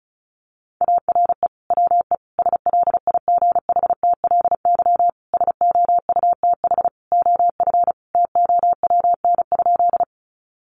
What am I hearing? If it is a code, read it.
Morse code, heard as a keyed tone, about 34 wpm, 700 Hz. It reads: ARE WE SLIGHTLY SOUTH OF TOWN?
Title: Sentences from Top 1300 Words 35wpm